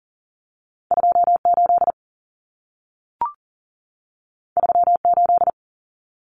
Morse code, heard as a keyed tone, about 40 wpm, 700 Hz. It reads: 28 38
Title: Common Cut Numbers - 2-Digit 40wpm